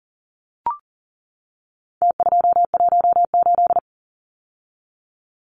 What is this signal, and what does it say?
Morse code, heard as a keyed tone, about 40 wpm, 700 Hz. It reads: T218
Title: Numbers - Contest Style - 4-Digit 40wpm